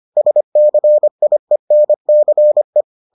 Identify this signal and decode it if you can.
Morse code, heard as a keyed tone, about 25 wpm, 600 Hz. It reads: SCIENCE